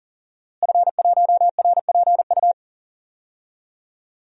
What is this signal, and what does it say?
Morse code, heard as a keyed tone, about 40 wpm, 700 Hz. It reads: F1RPU
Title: DXCC-100 Call Signs - 5 Characters - Repeated 2x - 40wpm